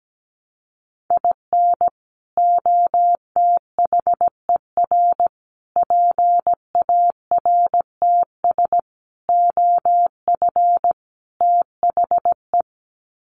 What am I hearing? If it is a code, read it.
Morse code, heard as a keyed tone, about 17 wpm, 700 Hz. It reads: IN OTHER PARTS OF THE
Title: Sets of 5 Words 17wpm